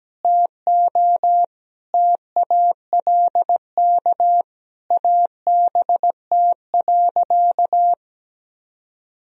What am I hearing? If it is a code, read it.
Morse code, heard as a keyed tone, about 17 wpm, 700 Hz. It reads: TO TALK ABT.
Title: Sentences from Top 400 Words and Abbreviations 17wpm